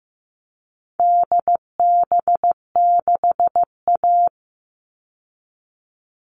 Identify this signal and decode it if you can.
Morse code, heard as a keyed tone, about 15 wpm, 700 Hz. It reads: DB6A
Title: All Cut Numbers - 4-Digit 15wpm